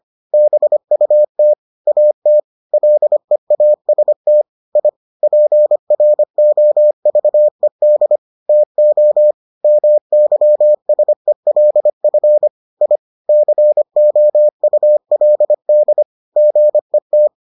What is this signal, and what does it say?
Morse code, heard as a keyed tone, about 25 wpm, 600 Hz. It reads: BUT AT LEAST I PROVED TO MYSELF I COULD GET